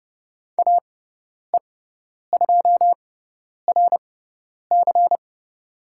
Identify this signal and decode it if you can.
Morse code, heard as a keyed tone, about 30 wpm, spaced as for 10 wpm, 700 Hz. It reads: AE2RC
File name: Lesson 16g - Letter c - Call Signs 30wpm